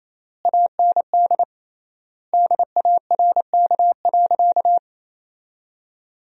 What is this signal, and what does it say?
Morse code, heard as a keyed tone, about 28 wpm, 700 Hz. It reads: AND DARK.
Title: 3x Word-Spacing - Sentences from Top 500 Words - 28wpm